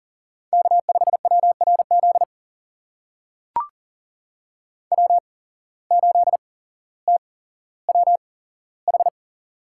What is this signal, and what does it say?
Morse code, heard as a keyed tone, about 39 wpm, spaced as for 17 wpm, 700 Hz. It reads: K5WRZ W8TWH
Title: USA Call Signs - 5 Characters - 8x Character Spacing - 40wpm